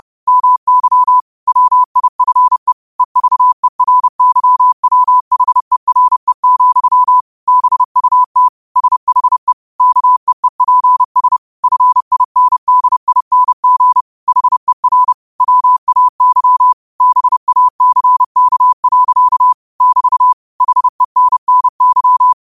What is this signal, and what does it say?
Morse code, heard as a keyed tone, about 30 wpm, 1000 Hz. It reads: MY WIFE EVERYWHERE, BUT SHE KEEPS FINDING HER WAY BACK. <BT> HENNY